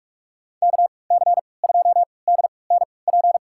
Morse code, heard as a keyed tone, about 46 wpm, spaced as for 26 wpm, 700 Hz. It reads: KC2DNP
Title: USA Call Signs - 6 Characters - 3x Character Spacing - 45wpm